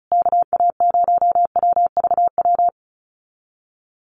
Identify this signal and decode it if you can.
Morse code, heard as a keyed tone, about 34 wpm, 700 Hz. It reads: KA0WVW
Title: USA Call Signs Repeated 3x 35wpm